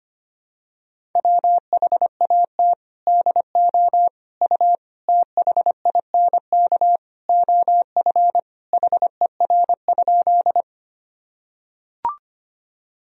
Morse code, heard as a keyed tone, about 25 wpm, 700 Hz. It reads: WHAT DO U THINK OF HER?
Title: Sentences from Top 200 Words and Abbreviations 25wpm